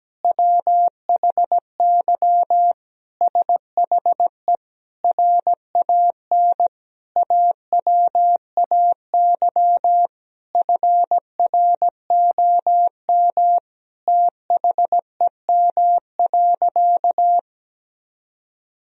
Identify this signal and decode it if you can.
Morse code, heard as a keyed tone, about 17 wpm, 700 Hz. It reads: WHY SHE RAN AWAY FROM THEM.
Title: Sentences from Top 1000 Words - Review Part 3 - 17wpm